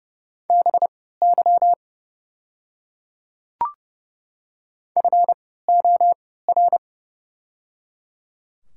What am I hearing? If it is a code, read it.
Morse code, heard as a keyed tone, about 30 wpm, spaced as for 17 wpm, 700 Hz. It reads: BY FOR
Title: 3x Character Spacing - Top 100 Words - 30wpm